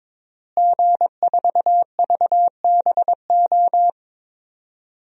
Morse code, heard as a keyed tone, about 22 wpm, 700 Hz. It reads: G4VBO
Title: DXCC-100 Call Signs - 5 Characters - Repeated 3x - 22wpm